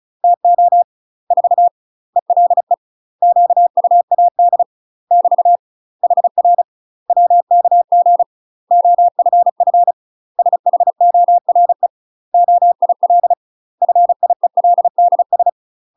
Morse code, heard as a keyed tone, about 34 wpm, 700 Hz. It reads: TO 4 ELE QUAD <BT> HR WKG OFF SHORE OIL FIELDS
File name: UK MASTER QSOs TRAINING/35WPM